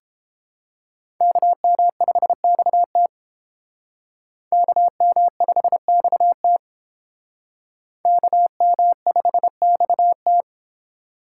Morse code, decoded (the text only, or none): KM5XT KM5XT KM5XT